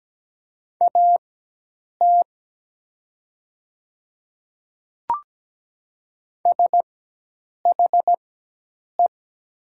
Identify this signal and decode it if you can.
Morse code, heard as a keyed tone, about 17 wpm, spaced as for 8 wpm, 700 Hz. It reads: AT SHE
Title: 4x Character Spacing - Top 100 Words - 17wpm